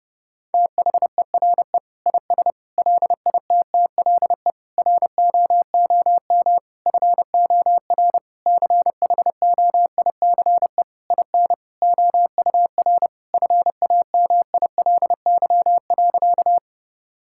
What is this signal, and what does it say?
Morse code, heard as a keyed tone, about 32 wpm, 700 Hz. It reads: THERE IS LITTLE ROOM FOR CHOICE IN OUR FAMILY.
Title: Sentences from Top 1000 Words - Review Part 6 - 30wpm